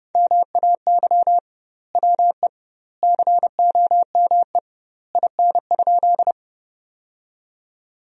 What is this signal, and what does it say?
Morse code, heard as a keyed tone, about 30 wpm, 700 Hz. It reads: MAY WE COME IN?
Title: 2x Word-Spacing - Sentences from Top 100 Words - 30wpm